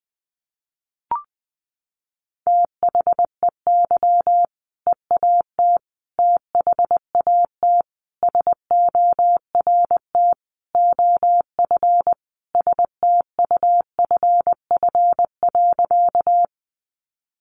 Morse code, heard as a keyed tone, about 20 wpm, 700 Hz. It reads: THEY EAT THAT SORT OF STUFF.
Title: Sentences from Top 1400 Words 20wpm